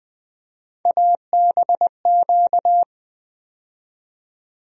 Morse code, heard as a keyed tone, about 20 wpm, 700 Hz. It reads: ABQ
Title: USA Call Sign Postfixes - 3 Letters 20wpm